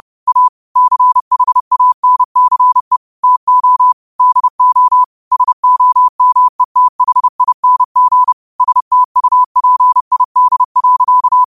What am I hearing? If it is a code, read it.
Morse code, heard as a keyed tone, about 30 wpm, 1000 Hz. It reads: A CHANCE TO DO SOMETHING STUPID.
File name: qotdfiles / 30wpm